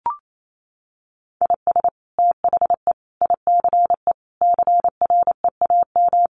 Morse code, heard as a keyed tone, about 28 wpm, 700 Hz. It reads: IS THE ICE CREAM
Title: Sentences from Top 1500 Words 28wpm